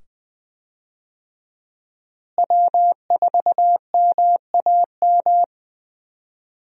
Morse code, heard as a keyed tone, about 20 wpm, 700 Hz. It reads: W4MAM
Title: USA Call Signs - 5 Characters - Repeated 2x - 20wpm